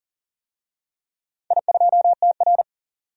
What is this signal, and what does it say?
Morse code, heard as a keyed tone, about 40 wpm, 700 Hz. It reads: I2TR
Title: DXCC-100 Call Signs - 4 Characters - 40wpm